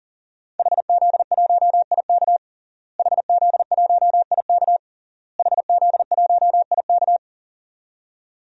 Morse code, heard as a keyed tone, about 37 wpm, 700 Hz. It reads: HZ1IK HZ1IK HZ1IK
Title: DXCC-100 Call Signs - 5 Characters - Repeated 3x - 40wpm